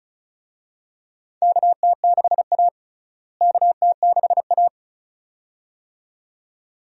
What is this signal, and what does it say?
Morse code, heard as a keyed tone, about 36 wpm, 700 Hz. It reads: KT6A KT6A
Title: USA Call Signs - 4 Characters - Repeated 2x - 35wpm